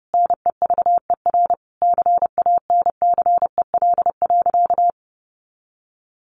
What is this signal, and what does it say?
Morse code, heard as a keyed tone, about 30 wpm, 700 Hz. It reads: NEVER CANCEL.
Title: Sentences from Top 1200 Words 30wpm